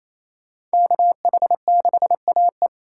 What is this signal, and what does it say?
Morse code, heard as a keyed tone, about 35 wpm, 700 Hz. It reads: KH6AE